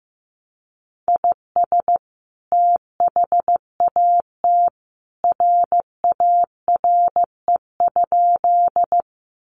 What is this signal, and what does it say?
Morse code, heard as a keyed tone, about 15 wpm, 700 Hz. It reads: IS THAT RARE?
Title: Sentences from Top 1100 Words 15wpm